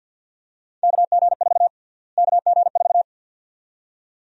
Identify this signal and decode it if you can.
Morse code, heard as a keyed tone, about 50 wpm, 700 Hz. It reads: KG4 KG4
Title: USA Call Sign Prefixes - Repeated 2x - 50wpm